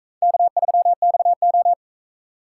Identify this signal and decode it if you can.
Morse code, heard as a keyed tone, about 42 wpm, 700 Hz. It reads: K3XO